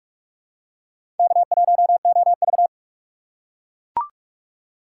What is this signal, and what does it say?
Morse code, heard as a keyed tone, about 43 wpm, 690 Hz. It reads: K1OV
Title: USA Call Signs - 4 Characters - 3x Character Spacing - 45wpm